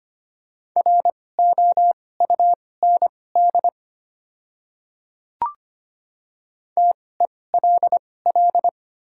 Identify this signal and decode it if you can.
Morse code, heard as a keyed tone, about 25 wpm, spaced as for 18 wpm, 700 Hz. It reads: ROUND TELL
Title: 2x Character Spacing - Top 200 Words - 25wpm